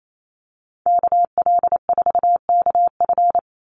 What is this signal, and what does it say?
Morse code, heard as a keyed tone, about 28 wpm, 700 Hz. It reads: KL4KF
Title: USA Call Signs 28wpm